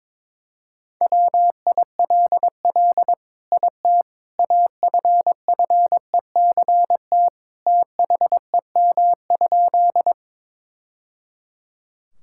Morse code, heard as a keyed tone, about 21 wpm, 700 Hz. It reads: WILL IT AFFECT THEM?
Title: Sentences from Top 1000 Words - Review Part 6 - 22wpm